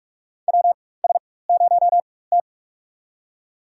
Morse code, heard as a keyed tone, about 44 wpm, spaced as for 21 wpm, 700 Hz. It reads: WS0T
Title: USA Call Signs - 4 Characters - 4x Character Spacing - 45wpm